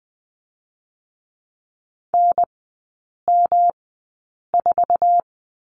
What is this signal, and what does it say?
Morse code, heard as a keyed tone, about 20 wpm, spaced as for 8 wpm, 700 Hz. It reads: NM4
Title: 138 Lesson 23e - Letter f - Three Characters 20wpm